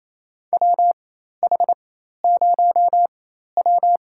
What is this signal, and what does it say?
Morse code, heard as a keyed tone, about 27 wpm, spaced as for 13 wpm, 700 Hz. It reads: WH0W
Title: USA Call Signs - 4 Characters - 4x Character Spacing - 28wpm